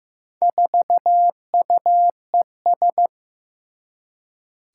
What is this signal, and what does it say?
Morse code, heard as a keyed tone, about 15 wpm, 700 Hz. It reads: 4UES
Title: Four Challenging Letter-Number Combinations 15wpm